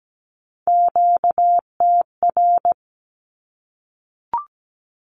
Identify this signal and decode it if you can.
Morse code, heard as a keyed tone, about 17 wpm, 700 Hz. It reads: QTR
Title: Q-Codes Repeated 3x 17wpm